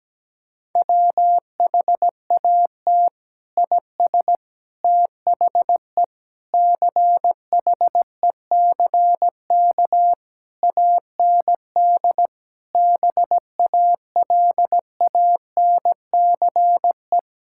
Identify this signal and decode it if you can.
Morse code, heard as a keyed tone, about 17 wpm, 700 Hz. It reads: WHAT IS THE CHECK AND BALANCE
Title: Sentences from Top 1000 Words - Review Part 6 - 17wpm